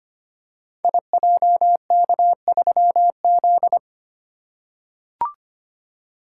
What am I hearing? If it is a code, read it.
Morse code, heard as a keyed tone, about 25 wpm, 700 Hz. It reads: IJK3Z
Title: Code Groups - 5 Letter:Numbers - 25wpm